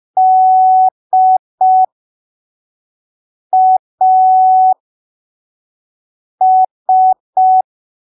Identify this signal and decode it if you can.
Morse code, heard as a keyed tone, about 5 wpm, spaced as for 3 wpm, 750 Hz. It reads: DAS
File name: arquivos / texto / texto 10 05ppm